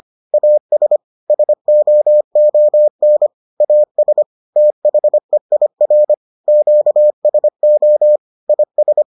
Morse code, heard as a keyed tone, about 25 wpm, 600 Hz. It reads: AS SOON AS THEIR QSO IS